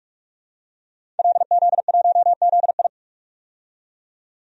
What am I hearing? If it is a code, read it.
Morse code, heard as a keyed tone, about 47 wpm, 700 Hz. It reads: LZ1ZI
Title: DXCC-100 Call Signs - 5 Characters - 45wpm